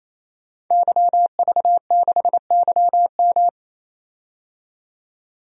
Morse code, heard as a keyed tone, about 28 wpm, 700 Hz. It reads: YV6YM
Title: DXCC-100 Call Signs - 5 Characters - 28wpm